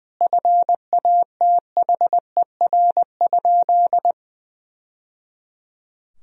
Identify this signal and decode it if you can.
Morse code, heard as a keyed tone, about 20 wpm, 700 Hz. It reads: FATHER?
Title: Sentences from Top 1000 Words - Review Part 3 - 20wpm